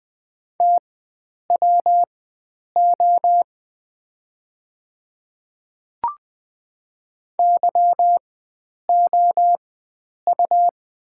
Morse code, heard as a keyed tone, about 20 wpm, spaced as for 9 wpm, 700 Hz. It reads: TWO YOU
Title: 4x Character Spacing - Top 100 Words - 20wpm